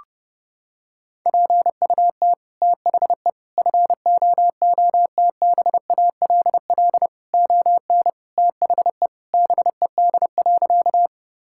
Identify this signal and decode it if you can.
Morse code, heard as a keyed tone, about 30 wpm, 700 Hz. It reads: PUT THE FOOTBALL ON THE BED.
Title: Sentences from Top 1600 Words 30wpm